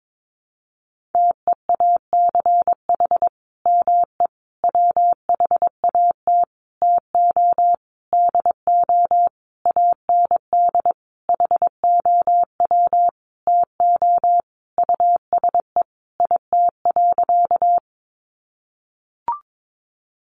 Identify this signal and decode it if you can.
Morse code, heard as a keyed tone, about 22 wpm, 700 Hz. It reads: TEACH ME WHAT TO DO AND HOW TO USE IT.
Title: Sentences from Top 500 Words 22wpm